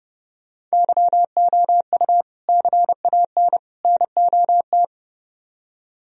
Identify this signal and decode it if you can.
Morse code, heard as a keyed tone, about 38 wpm, 700 Hz. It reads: YOU CAN NOT